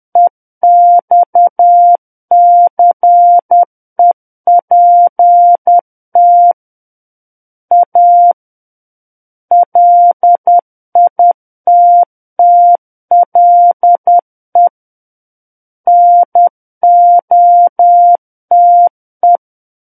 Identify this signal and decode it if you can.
Morse code, heard as a keyed tone, about 10 wpm, 700 Hz. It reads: EXCEPT A LITTLE NOTE
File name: Book One / 01-Chapter 1 - 10 WPM